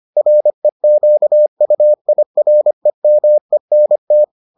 Morse code, heard as a keyed tone, about 25 wpm, 600 Hz. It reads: REQUIREMENT